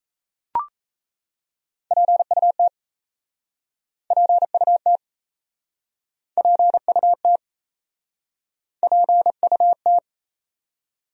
Morse code, heard as a keyed tone, about 39 wpm, 700 Hz. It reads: PUT PUT PUT PUT